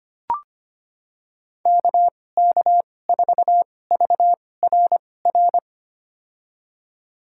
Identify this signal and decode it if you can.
Morse code, heard as a keyed tone, about 25 wpm, spaced as for 18 wpm, 700 Hz. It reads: KK4VRR
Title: USA Call Signs - 6 Characters - 2x Character Spacing - 25wpm